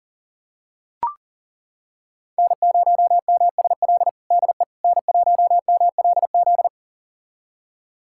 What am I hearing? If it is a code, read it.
Morse code, heard as a keyed tone, about 40 wpm, 700 Hz. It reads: N0MSL DE N1MLZ
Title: USA Callsign DE Callsign - 5 Characters - 40wpm